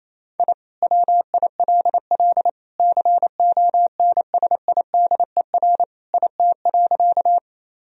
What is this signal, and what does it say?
Morse code, heard as a keyed tone, about 27 wpm, 700 Hz. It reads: I WILL CONSIDER IT.
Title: Sentences from Top 1000 Words - Review Part 4 - 28wpm